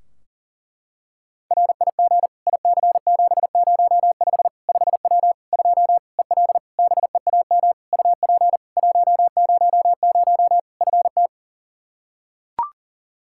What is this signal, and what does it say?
Morse code, heard as a keyed tone, about 37 wpm, 700 Hz. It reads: RIG IC705 5W 2 EL BEAM UP 100 FT
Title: K1USN SST QSO Practice 40wpm